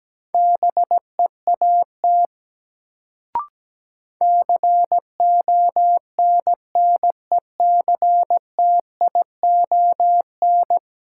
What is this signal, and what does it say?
Morse code, heard as a keyed tone, about 17 wpm, 700 Hz. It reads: BEAT CONNECTION
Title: Sentences from Top 1300 Words - New Word Drill Alt - 17wpm